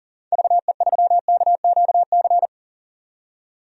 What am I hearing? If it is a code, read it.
Morse code, heard as a keyed tone, about 40 wpm, 700 Hz. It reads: VE3KQC
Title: DXCC-100 Call Signs - 6 Characters - 40wpm